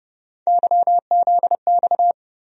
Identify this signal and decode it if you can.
Morse code, heard as a keyed tone, about 32 wpm, 700 Hz. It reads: YZX
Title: USA Call Sign Postfixes - 3 Letters 30wpm